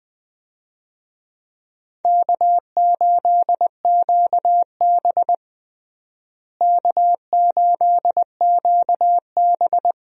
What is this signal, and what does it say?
Morse code, heard as a keyed tone, about 21 wpm, 700 Hz. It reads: K8QB K8QB
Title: USA Call Signs - 4 Characters - Repeated 3x - 20wpm